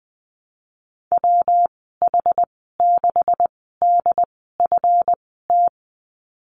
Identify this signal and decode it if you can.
Morse code, heard as a keyed tone, about 20 wpm, spaced as for 15 wpm, 700 Hz. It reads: WH6DFT
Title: 161 Lesson 26g - Letter g - Call Signs 20wpm